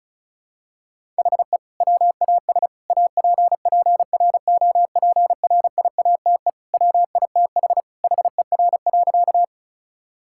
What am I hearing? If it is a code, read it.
Morse code, heard as a keyed tone, about 34 wpm, 700 Hz. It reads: HE WAS APPROPRIATE WITH HER.
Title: Sentences from Top 1000 Words - Review Part 5 - 35wpm